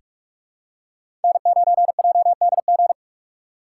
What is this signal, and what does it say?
Morse code, heard as a keyed tone, about 46 wpm, 700 Hz. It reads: N9JDG
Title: USA Call Signs - 3x Character Spacing - 45wpm